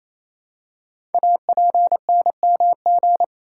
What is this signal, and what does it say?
Morse code, heard as a keyed tone, about 27 wpm, 700 Hz. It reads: APNMG
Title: Code Groups - 5 Letters - 28wpm